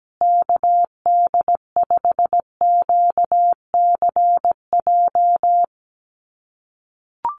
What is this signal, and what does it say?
Morse code, heard as a keyed tone, about 17 wpm, 700 Hz. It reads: KD5QCJ
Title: USA Call Signs Repeated 3x 17wpm